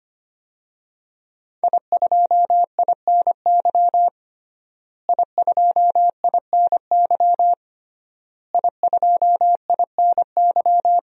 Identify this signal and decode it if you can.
Morse code, heard as a keyed tone, about 25 wpm, 700 Hz. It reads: I2INY I2INY I2INY
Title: DXCC-100 Call Signs - 5 Characters - Repeated 3x - 25wpm